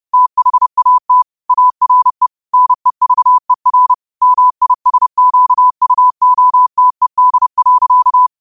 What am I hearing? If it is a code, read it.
Morse code, heard as a keyed tone, about 30 wpm, 1000 Hz. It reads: THAT ARE NEVER MISQUOTED.